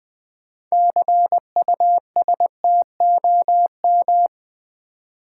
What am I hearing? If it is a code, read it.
Morse code, heard as a keyed tone, about 20 wpm, 700 Hz. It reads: CUSTOM
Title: 6 Letter Words - First 3 Letters Given - 20wpm